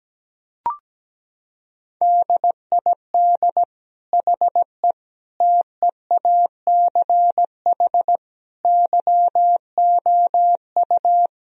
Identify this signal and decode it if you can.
Morse code, heard as a keyed tone, about 17 wpm, 700 Hz. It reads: DID HE TEACH YOU
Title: Sentences from Top 1000 Words - Review Part 2 - 17wpm